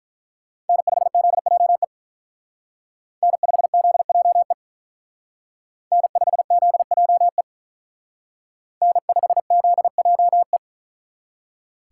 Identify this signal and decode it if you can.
Morse code, heard as a keyed tone, about 43 wpm, 700 Hz. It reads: N5ZJE N5ZJE N5ZJE N5ZJE
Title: Speed Racing - USA 1x3 Call Signs 35wpm